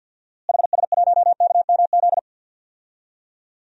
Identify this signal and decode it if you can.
Morse code, heard as a keyed tone, about 51 wpm, 700 Hz. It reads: HS1KMZ
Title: DXCC-100 Call Signs - 6 Characters - Repeated 2x - 50wpm